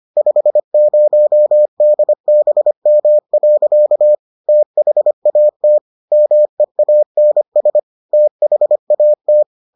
Morse code, heard as a keyed tone, about 25 wpm, 600 Hz. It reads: 50DBM. THAT MEANS THAT